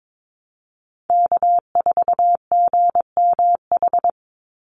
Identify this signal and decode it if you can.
Morse code, heard as a keyed tone, about 22 wpm, 700 Hz. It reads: K4GMH
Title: Call Signs Repeated 2x - 22wpm